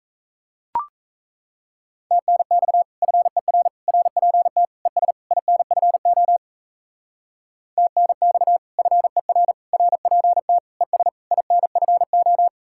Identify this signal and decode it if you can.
Morse code, heard as a keyed tone, about 34 wpm, 700 Hz. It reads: TNX FER RPT ES INFO TNX FER RPT ES INFO
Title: Speed-Racing Common QSO Phrases Encore - 28wpm